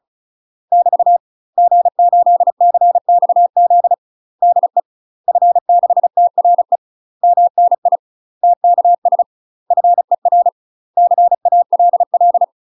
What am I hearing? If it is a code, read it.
Morse code, heard as a keyed tone, about 34 wpm, 700 Hz. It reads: <BT> G8CXZ DE F6TRE MNI TKS FER CALL
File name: UK MASTER QSOs TRAINING / 35WPM / UK MASTER QSOs 0025